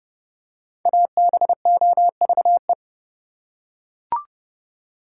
Morse code, heard as a keyed tone, about 30 wpm, 700 Hz. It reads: ABOVE
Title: Five Letter Words - 30wpm